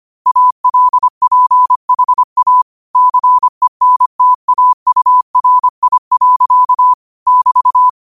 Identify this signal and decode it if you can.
Morse code, heard as a keyed tone, about 25 wpm, 1000 Hz. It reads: ALPHA CENTAURI. <BT>
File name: qotdfiles / 25wpm